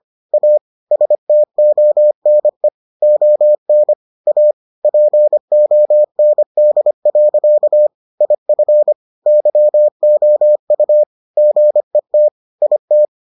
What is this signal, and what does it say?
Morse code, heard as a keyed tone, about 25 wpm, 600 Hz. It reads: A STONE ON A POND. IF YOU GET IT